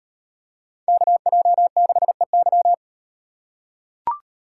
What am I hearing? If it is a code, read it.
Morse code, heard as a keyed tone, about 38 wpm, 700 Hz. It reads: KJ6EY